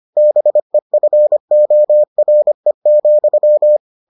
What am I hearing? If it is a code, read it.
Morse code, heard as a keyed tone, about 25 wpm, 600 Hz. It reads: BEFORE,